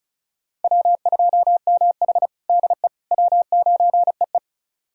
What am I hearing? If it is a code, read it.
Morse code, heard as a keyed tone, about 36 wpm, 700 Hz. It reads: W2MH DE W9EE
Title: USA Callsign DE Callsign - 4 Characters - 35wpm